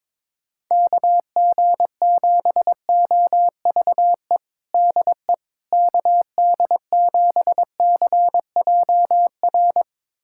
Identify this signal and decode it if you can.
Morse code, heard as a keyed tone, about 22 wpm, 700 Hz. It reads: KG7OVE DE KD7CJR
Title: USA Callsign DE Callsign 22wpm